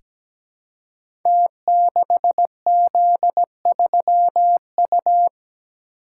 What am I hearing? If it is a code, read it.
Morse code, heard as a keyed tone, about 17 wpm, 700 Hz. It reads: T6Z3U
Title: Code Groups - 5 Letter:Numbers - 17wpm